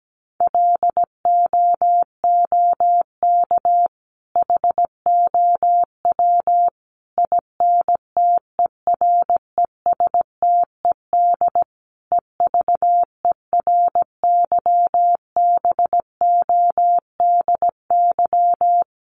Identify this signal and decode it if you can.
Morse code, heard as a keyed tone, about 17 wpm, 700 Hz. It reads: LOOK HOW INTERESTED EVERYBODY
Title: Sentences from Top 1400 Words 17wpm